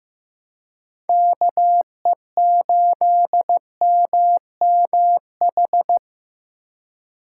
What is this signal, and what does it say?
Morse code, heard as a keyed tone, about 15 wpm, 700 Hz. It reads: KE8MMH
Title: USA Call Signs - 6 Characters - 3x Character Spacing - 15wpm